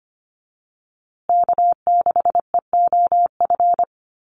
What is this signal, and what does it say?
Morse code, heard as a keyed tone, about 25 wpm, 700 Hz. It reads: K6EOF
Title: USA Call Signs Repeated 3x 25wpm